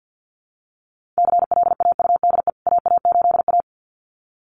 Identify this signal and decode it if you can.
Morse code, heard as a keyed tone, about 48 wpm, 700 Hz. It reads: CLAUDE AA8A
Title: CWT Exchange 50wpm